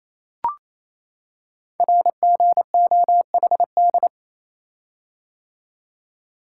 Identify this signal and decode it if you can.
Morse code, heard as a keyed tone, about 27 wpm, 700 Hz. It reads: RGOHD
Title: Code Groups - 5 Letters - 28wpm